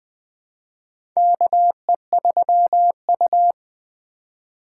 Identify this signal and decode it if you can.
Morse code, heard as a keyed tone, about 20 wpm, 700 Hz. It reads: KE3U
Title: USA Call Signs - 4 Characters - 4x Character Spacing - 20wpm